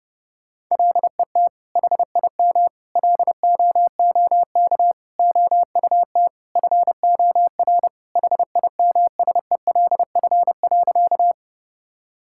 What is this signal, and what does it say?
Morse code, heard as a keyed tone, about 30 wpm, 700 Hz. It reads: LET HIM LOOK OUT FOR HIMSELF.
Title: Sentences from Top 1600 Words 30wpm